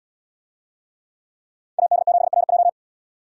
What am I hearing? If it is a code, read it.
Morse code, heard as a keyed tone, about 77 wpm, 700 Hz. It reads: AD7RJ